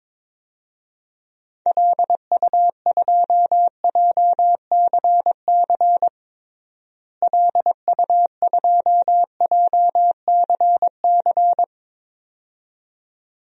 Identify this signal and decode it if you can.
Morse code, heard as a keyed tone, about 22 wpm, 700 Hz. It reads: LU2JCC LU2JCC
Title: DXCC-100 Call Signs - 6 Characters - Repeated 2x - 22wpm